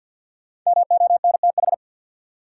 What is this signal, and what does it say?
Morse code, heard as a keyed tone, about 53 wpm, 700 Hz. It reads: MONTH
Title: Sets of 5 Words Encore - New Word Drill - 50wpm